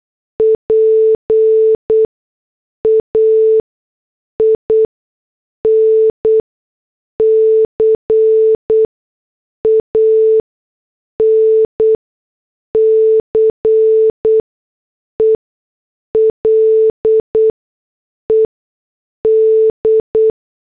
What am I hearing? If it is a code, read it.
Morse code, heard as a keyed tone, about 8 wpm, spaced as for 6 wpm, 440 Hz. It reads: PAIN CANCELED
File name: morse / headlines